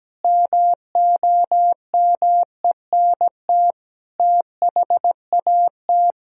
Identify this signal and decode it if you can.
Morse code, heard as a keyed tone, about 17 wpm, 700 Hz. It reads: MOMENT THAT
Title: Sentences from Top 1000 Words - Review Part 3 - 17wpm